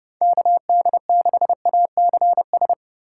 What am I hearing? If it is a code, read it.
Morse code, heard as a keyed tone, about 32 wpm, 700 Hz. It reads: KD6ACS
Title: USA Call Signs - 4x Character Spacing - 30wpm